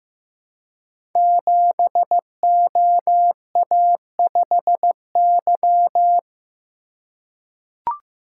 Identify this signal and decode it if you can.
Morse code, heard as a keyed tone, about 15 wpm, 700 Hz. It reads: 7OA5Y
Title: Code Groups - 5 Letter:Numbers - 15wpm